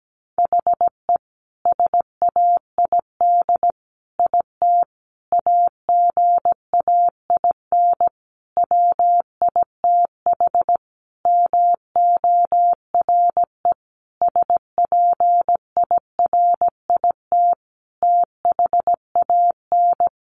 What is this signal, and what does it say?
Morse code, heard as a keyed tone, about 17 wpm, 700 Hz. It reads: HE SAID IT AGAIN WITH MORE SPIRIT THAN
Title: Sentences from Top 1400 Words 17wpm